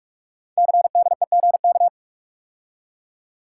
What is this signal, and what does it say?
Morse code, heard as a keyed tone, about 44 wpm, 700 Hz. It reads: CDEGK
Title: Code Groups - 5 Letters - 45wpm